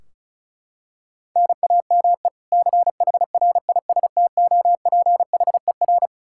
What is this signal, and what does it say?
Morse code, heard as a keyed tone, about 34 wpm, 700 Hz. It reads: NAME CHRISTOPHER
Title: Practice QSO - Exchange 1 - Beginning - 35wpm